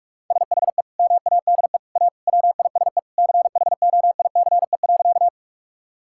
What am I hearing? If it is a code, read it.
Morse code, heard as a keyed tone, about 46 wpm, 700 Hz. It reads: SHE MADE A WISE CHOICE.
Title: Sentences from Top 1600 Words 45wpm